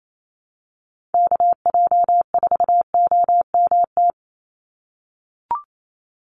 Morse code, heard as a keyed tone, about 28 wpm, 700 Hz. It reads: KJ4OMT
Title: USA Call Signs 28wpm